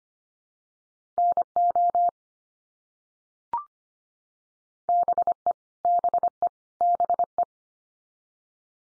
Morse code, heard as a keyed tone, about 25 wpm, 700 Hz. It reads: NO BE BE BE
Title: Top 100 Words 25wpm x3